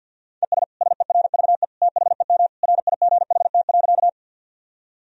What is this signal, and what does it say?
Morse code, heard as a keyed tone, about 51 wpm, 700 Hz. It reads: ES SERVE THEM RIGHT.
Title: Sentences from Top 400 Words and Abbreviations 50wpm